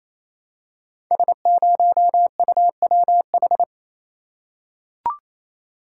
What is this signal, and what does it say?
Morse code, heard as a keyed tone, about 27 wpm, 700 Hz. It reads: S0UWH
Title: Code Groups - 5 Letter:Numbers - 28wpm